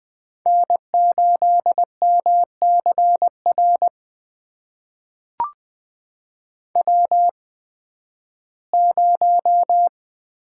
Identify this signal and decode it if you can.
Morse code, heard as a keyed tone, about 20 wpm, spaced as for 13 wpm, 700 Hz. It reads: N8MCR W0
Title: USA Call Signs - 5 Characters - 8x Character Spacing - 20wpm